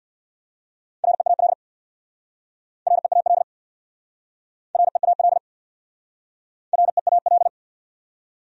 Morse code, heard as a keyed tone, about 67 wpm, 690 Hz. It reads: REAL REAL REAL REAL